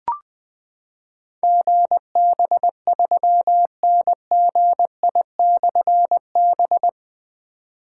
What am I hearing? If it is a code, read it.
Morse code, heard as a keyed tone, about 20 wpm, 700 Hz. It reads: GB3NGI/B
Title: Difficult Call Signs 20wpm